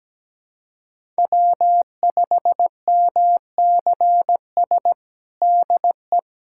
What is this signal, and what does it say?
Morse code, heard as a keyed tone, about 17 wpm, 700 Hz. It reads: W5MCS DE
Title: USA Callsign DE Callsign - 5 Characters - 17wpm